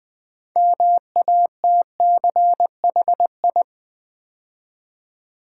Morse code, heard as a keyed tone, about 20 wpm, 700 Hz. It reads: MATCHI
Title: Build A Word - 3 Letters At A Time - 8-Letter Words - 20wpm